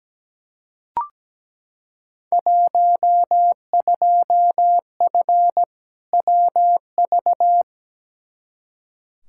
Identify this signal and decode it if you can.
Morse code, heard as a keyed tone, about 17 wpm, 700 Hz. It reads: 12F WV
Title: Field Day Exchanges Encore - 17wpm